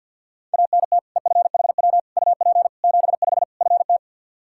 Morse code, heard as a keyed tone, about 49 wpm, 700 Hz. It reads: ANT EFHW UP 75 FT
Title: K1USN SST QSO Practice 50wpm